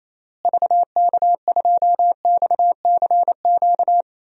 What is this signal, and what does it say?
Morse code, heard as a keyed tone, about 28 wpm, 700 Hz. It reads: VK2XCQ
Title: DXCC-100 Call Signs - 6 Characters - Repeated 3x - 28wpm